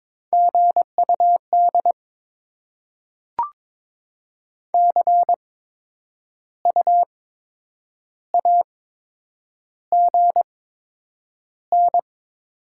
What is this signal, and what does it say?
Morse code, heard as a keyed tone, about 22 wpm, spaced as for 8 wpm, 700 Hz. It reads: GUD CU AGN
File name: Top 20 QSO Elements - 8x Character Spacing - 22wpm